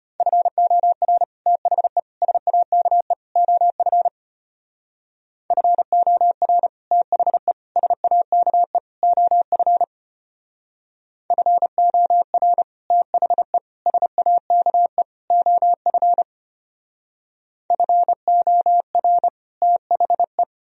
Morse code, heard as a keyed tone, about 32 wpm, 700 Hz. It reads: FOR THE SAKE OF FOR THE SAKE OF FOR THE SAKE OF FOR THE